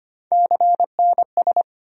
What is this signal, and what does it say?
Morse code, heard as a keyed tone, about 25 wpm, 700 Hz. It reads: CNS
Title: USA Call Sign Postfixes - 3 Letters 25wpm